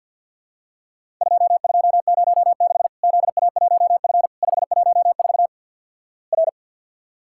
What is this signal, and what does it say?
Morse code, heard as a keyed tone, about 50 wpm, 700 Hz. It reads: 2206 ZA1F 514 R
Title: Worked All Europe DX Contest - QTC Exchange with Repeats - 50wpm